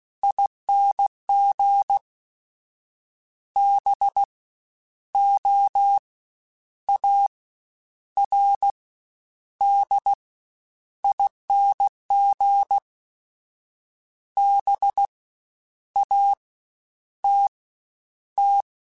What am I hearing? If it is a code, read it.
Morse code, heard as a keyed tone, about 16 wpm, 800 Hz. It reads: ING B O A R D ING B A T T
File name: Special INGs-mp3s / Special INGs-mp3s / SlowToFaster ing @15